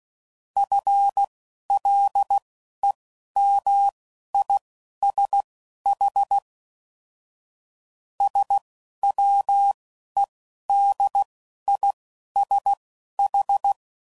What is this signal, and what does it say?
Morse code, heard as a keyed tone, about 16 wpm, spaced as for 11 wpm, 800 Hz. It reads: FLEMISH SWEDISH